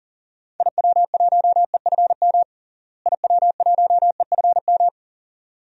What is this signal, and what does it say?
Morse code, heard as a keyed tone, about 40 wpm, 700 Hz. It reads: IW1EFM IW1EFM
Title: DXCC-100 Call Signs - 6 Characters - Repeated 2x - 40wpm